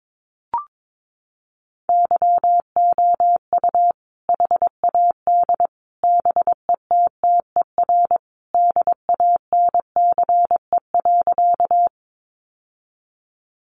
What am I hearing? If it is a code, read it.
Morse code, heard as a keyed tone, about 22 wpm, 700 Hz. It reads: YOU HAD BETTER DANCE.
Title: Sentences from Top 700 Words 22wpm